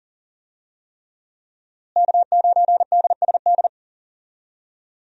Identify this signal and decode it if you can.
Morse code, heard as a keyed tone, about 39 wpm, 700 Hz. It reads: K9DSD
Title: USA Call Signs - 5 Characters - Repeated 3x - 40wpm